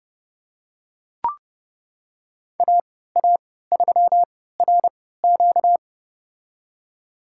Morse code, heard as a keyed tone, about 32 wpm, spaced as for 17 wpm, 700 Hz. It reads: AA3RQ
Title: USA Call Signs - 5 Characters - 3x Character Spacing - 30wpm